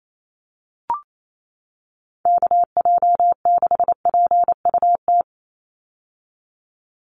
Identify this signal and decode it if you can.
Morse code, heard as a keyed tone, about 28 wpm, 700 Hz. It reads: KJ6PUT
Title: USA Call Signs 28wpm